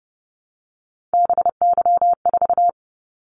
Morse code, heard as a keyed tone, about 30 wpm, 700 Hz. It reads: BY4
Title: Callsign Prefixes for Top 100 DXCC - 30wpm